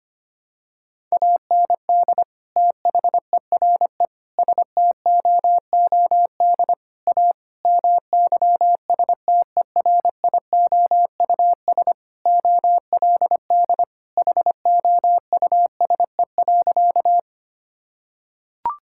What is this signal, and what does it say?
Morse code, heard as a keyed tone, about 25 wpm, 700 Hz. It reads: AND THERE STOOD A MYSTERIOUS OLD HOUSE.
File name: practice / 25wpm / Sentences with 2-Word Phrases - 25wpm